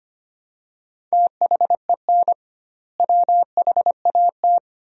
Morse code, heard as a keyed tone, about 25 wpm, 700 Hz. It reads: THEN WHAT
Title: 2x Word-Spacing - Sentences from Top 400 Words - 25wpm